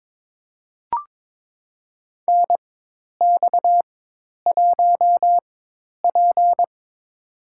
Morse code, heard as a keyed tone, about 22 wpm, spaced as for 10 wpm, 700 Hz. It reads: NX1P
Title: USA Call Signs - 4 Characters - 4x Character Spacing - 22wpm